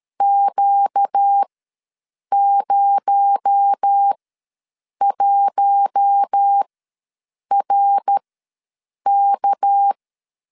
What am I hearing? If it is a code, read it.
Morse code, heard as a keyed tone, about 13 wpm, spaced as for 7 wpm, 800 Hz. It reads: Q01RK
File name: LEZIONI / 6 PKX0 / 6 PKX0 ws 12wpm 3448